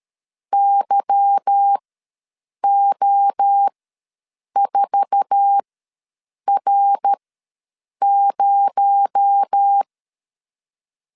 Morse code, heard as a keyed tone, about 13 wpm, spaced as for 7 wpm, 800 Hz. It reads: YO4R0
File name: LEZIONI / 9 28 ws 12wpm 3354